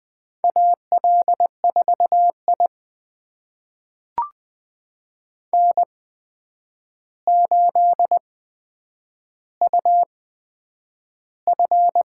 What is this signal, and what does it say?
Morse code, heard as a keyed tone, about 20 wpm, spaced as for 9 wpm, 700 Hz. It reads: AL4I N8UF
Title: USA Call Signs - 4 Characters - 8x Character Spacing - 20wpm